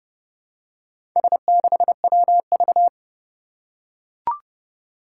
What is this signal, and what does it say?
Morse code, heard as a keyed tone, about 28 wpm, 690 Hz. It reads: S6WV
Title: Four Challenging Letter-Number Combinations 30wpm